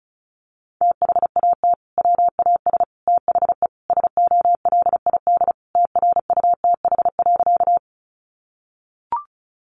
Morse code, heard as a keyed tone, about 34 wpm, 700 Hz. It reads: THAT WAS THE SOLID TRUTH.
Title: Sentences from Top 1500 Words 35wpm